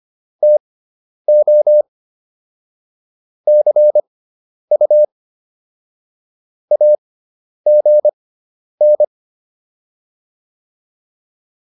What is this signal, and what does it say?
Morse code, heard as a keyed tone, about 25 wpm, spaced as for 10 wpm, 600 Hz. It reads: TO CU AGN